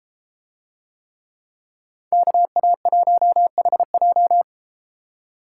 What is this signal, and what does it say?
Morse code, heard as a keyed tone, about 33 wpm, 700 Hz. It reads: KA1HJ